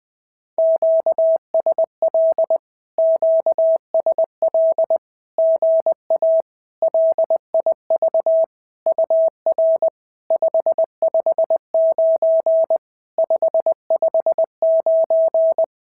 Morse code, heard as a keyed tone, about 20 wpm, 650 Hz. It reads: QSL QSL GA LIV UR 559 559
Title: SOTA QSO Practice 20wpm